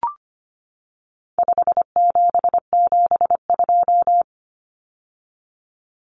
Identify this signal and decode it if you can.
Morse code, heard as a keyed tone, about 25 wpm, 700 Hz. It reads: 5772
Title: Common Cut Numbers - 4-Digit 25wpm